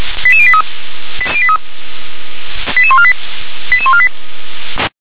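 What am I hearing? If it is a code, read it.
Five-tone selective call (ZVEI), heard as quick runs of five tones, tones of about 80 ms. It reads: 88083 88148 88148
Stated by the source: ZVEI five-tone selective call 88634, received by SDR.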